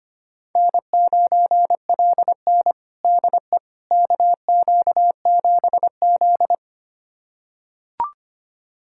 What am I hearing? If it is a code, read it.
Morse code, heard as a keyed tone, about 25 wpm, 700 Hz. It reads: N9LN DE KQ7Z
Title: USA Callsign DE Callsign - 4 Characters - 25wpm